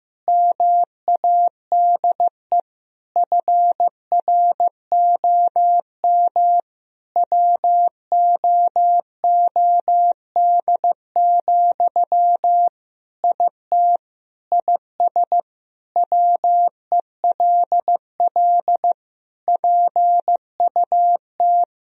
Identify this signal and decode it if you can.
Morse code, heard as a keyed tone, about 15 wpm, 700 Hz. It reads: MADE FROM WOOD, IT IS WELL PUT
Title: Sentences from Top 1000 Words - Review Part 3 - 15wpm